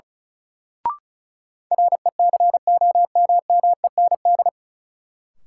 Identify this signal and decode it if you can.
Morse code, heard as a keyed tone, about 34 wpm, 700 Hz. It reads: RECOMMEND
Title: Sentences from Top 1200 Words - New Word Drill - 35wpm